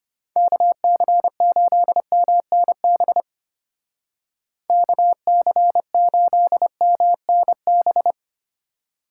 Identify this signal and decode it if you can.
Morse code, heard as a keyed tone, about 32 wpm, 700 Hz. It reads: KC8MNB KC8MNB